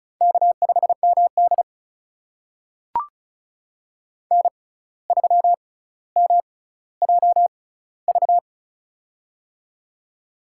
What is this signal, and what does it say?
Morse code, heard as a keyed tone, about 36 wpm, spaced as for 17 wpm, 700 Hz. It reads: K5MD N3MJV
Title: USA Call Signs - 6x Character Spacing - 35wpm